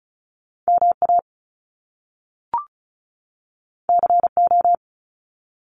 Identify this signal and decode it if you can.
Morse code, heard as a keyed tone, about 35 wpm, 700 Hz. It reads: MA CO
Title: US States 35wpm